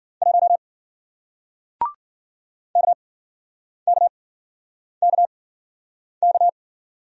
Morse code, heard as a keyed tone, about 53 wpm, spaced as for 9 wpm, 700 Hz. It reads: 1 KKKK